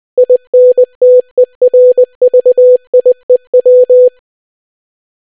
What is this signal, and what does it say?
Morse code, heard as a keyed tone, about 20 wpm, 500 Hz. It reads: INTERVIEW